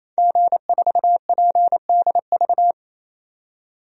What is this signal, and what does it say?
Morse code, heard as a keyed tone, about 28 wpm, 700 Hz. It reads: G4PDV
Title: DXCC-100 Call Signs - 5 Characters - 28wpm